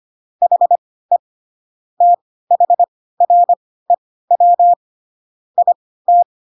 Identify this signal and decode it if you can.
Morse code, heard as a keyed tone, about 25 wpm, spaced as for 16 wpm, 700 Hz. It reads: HE THREW IT
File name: Pinocchio / Pinocchio ch1 10 25 ik1hge